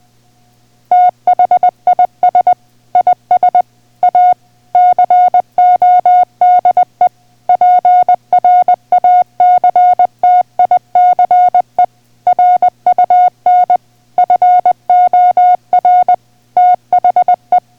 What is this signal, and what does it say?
Morse code, heard as a keyed tone, about 20 wpm, 720 Hz. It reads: THIS IS A CODE PRACTICE RUN FOR THE